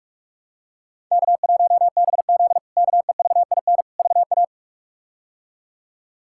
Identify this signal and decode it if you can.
Morse code, heard as a keyed tone, about 46 wpm, 700 Hz. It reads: K1BZ KEVIN VA
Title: K1USN SST QSO Practice - 45wpm